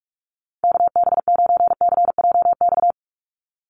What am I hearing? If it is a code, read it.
Morse code, heard as a keyed tone, about 46 wpm, 700 Hz. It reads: KB9CJX
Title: USA Call Signs 45wpm